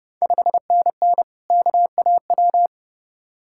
Morse code, heard as a keyed tone, about 30 wpm, 700 Hz. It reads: 5NN KAW
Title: Ontario QSO Party Exchange 30wpm